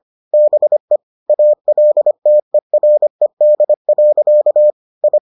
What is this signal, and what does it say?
Morse code, heard as a keyed tone, about 25 wpm, 600 Hz. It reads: BE ALTERED. I